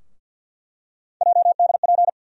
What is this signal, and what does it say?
Morse code, heard as a keyed tone, about 48 wpm, 700 Hz. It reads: JDP
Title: USA Call Sign Postfixes - 3 Letters 50wpm